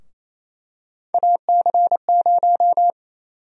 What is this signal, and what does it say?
Morse code, heard as a keyed tone, about 27 wpm, 700 Hz. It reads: AC0
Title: USA Call Sign Prefixes - 28wpm